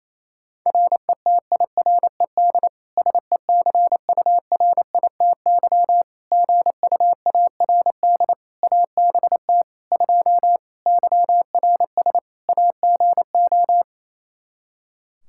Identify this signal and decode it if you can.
Morse code, heard as a keyed tone, about 28 wpm, 700 Hz. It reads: RETIRED SECURITY GUARD ABT 2 YRS AGO
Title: Practice QSO - Exchange 3 - More Info - 28wpm